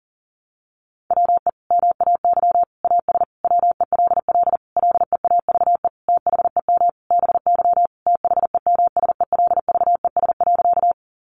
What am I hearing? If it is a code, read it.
Morse code, heard as a keyed tone, about 40 wpm, 700 Hz. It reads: WE MAY AS WELL LEAVE THEM BY THEMSELVES.
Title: Sentences from Top 800 Words 40wpm